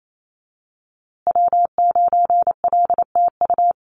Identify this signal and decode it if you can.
Morse code, heard as a keyed tone, about 28 wpm, 700 Hz. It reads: W9LTU
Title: USA Call Signs Repeated 3x 28wpm